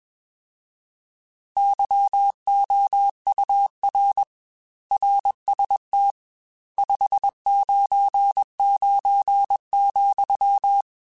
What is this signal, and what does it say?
Morse code, heard as a keyed tone, about 21 wpm, 800 Hz. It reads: YOUR RST 599,
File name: RSTs / RST-mp3s / Your RSTs @20